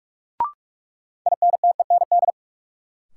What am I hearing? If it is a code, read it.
Morse code, heard as a keyed tone, about 44 wpm, 700 Hz. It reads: INTEND
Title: Sentences from Top 1600 Words - New Word Drill Alt - 45wpm